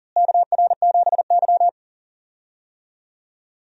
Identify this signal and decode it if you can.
Morse code, heard as a keyed tone, about 40 wpm, 700 Hz. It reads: KR7Y
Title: USA Call Signs - 4 Characters - Repeated 3x - 40wpm